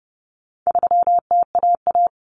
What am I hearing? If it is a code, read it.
Morse code, heard as a keyed tone, about 30 wpm, 700 Hz. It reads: 3TAA
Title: Common Cut Numbers - 4-Digit 30wpm